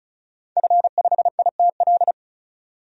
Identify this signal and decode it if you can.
Morse code, heard as a keyed tone, about 34 wpm, 700 Hz. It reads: F5ITL
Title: DXCC-100 Call Signs - 5 Characters - 35wpm